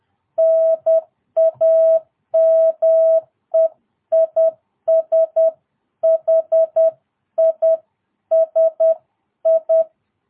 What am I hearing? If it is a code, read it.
Morse code, heard as a keyed tone, about 10 wpm, 650 Hz. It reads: NAME IS HISI
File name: jh8cbh / 2022cw